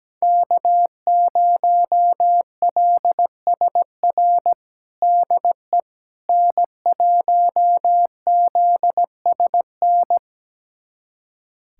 Morse code, heard as a keyed tone, about 17 wpm, 700 Hz. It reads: K0LSR DE N1ZSN
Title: USA Callsign DE Callsign - 5 Characters - 17wpm